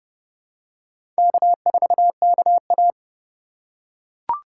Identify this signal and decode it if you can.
Morse code, heard as a keyed tone, about 32 wpm, 700 Hz. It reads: K4KA
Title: USA Call Signs - 2x Character Spacing - 30wpm